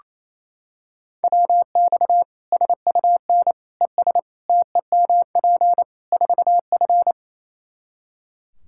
Morse code, heard as a keyed tone, about 28 wpm, 700 Hz. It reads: WX SUN ES TEMP 4F
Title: Practice QSO - Exchange 2 - Info - 28wpm